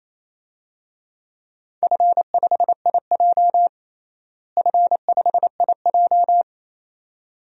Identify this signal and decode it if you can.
Morse code, heard as a keyed tone, about 28 wpm, 700 Hz. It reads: F5IJ F5IJ
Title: DXCC-100 Call Signs - 4 Characters - Repeated 2x - 28wpm